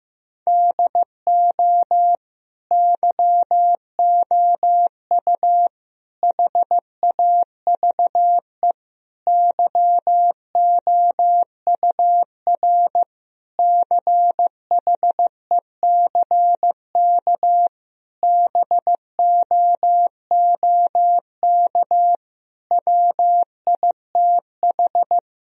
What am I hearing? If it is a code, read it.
Morse code, heard as a keyed tone, about 15 wpm, 700 Hz. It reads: DO YOU HAVE YOUR CHECK BOOK WITH
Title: Sentences from Top 1000 Words - Review Part 3 - 15wpm